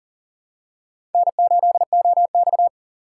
Morse code, heard as a keyed tone, about 38 wpm, 700 Hz. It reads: N8OX
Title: USA Call Signs - 6x Character Spacing - 40wpm